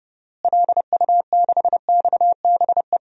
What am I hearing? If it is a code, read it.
Morse code, heard as a keyed tone, about 32 wpm, 700 Hz. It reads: LU6XBE
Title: DXCC-100 Call Signs - 6 Characters - Repeated 2x - 30wpm